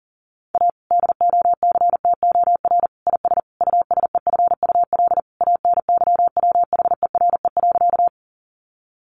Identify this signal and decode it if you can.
Morse code, heard as a keyed tone, about 37 wpm, 700 Hz. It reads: A DOCTOR IS USEFUL ANYWHERE.
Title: Sentences from Top 1500 Words 40wpm